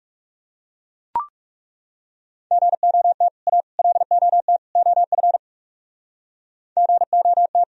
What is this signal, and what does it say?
Morse code, heard as a keyed tone, about 38 wpm, 700 Hz. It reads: GOT A LOT OF GOT